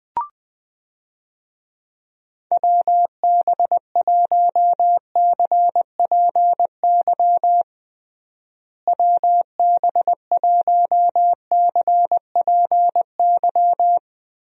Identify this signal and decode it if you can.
Morse code, heard as a keyed tone, about 20 wpm, 700 Hz. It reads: WB1CPY WB1CPY
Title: USA Call Signs - 6 Characters - Repeated 2x - 20wpm